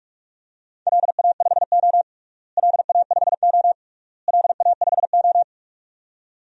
Morse code, heard as a keyed tone, about 46 wpm, 700 Hz. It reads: LA5O LA5O LA5O
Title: DXCC-100 Call Signs - 4 Characters - Repeated 3x - 45wpm